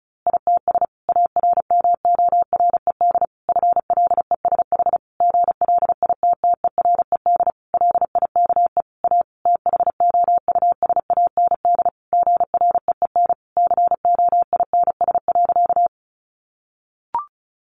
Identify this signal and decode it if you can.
Morse code, heard as a keyed tone, about 34 wpm, 700 Hz. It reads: ITS ARMORED FLESH GLITTERED LIKE A THOUSAND GREEN COINS.
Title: A Sound Of Thunder 35wpm